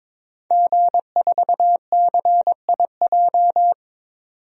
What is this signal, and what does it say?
Morse code, heard as a keyed tone, about 22 wpm, 700 Hz. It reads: G4CIJ
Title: DXCC-100 Call Signs - 5 Characters - 22wpm